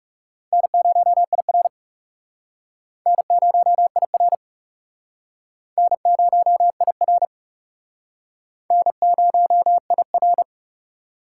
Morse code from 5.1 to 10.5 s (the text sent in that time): N0IR N0IR